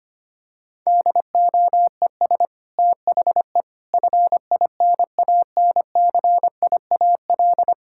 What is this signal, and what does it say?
Morse code, heard as a keyed tone, about 25 wpm, 700 Hz. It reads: DOES THE FINANCIAL
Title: Sentences from Top 1000 Words - Review Part 5 - 25wpm